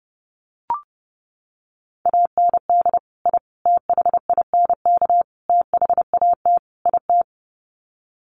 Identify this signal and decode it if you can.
Morse code, heard as a keyed tone, about 30 wpm, 700 Hz. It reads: AND I THINK THAT IT
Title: Sets of 5 Words Encore2 30wpm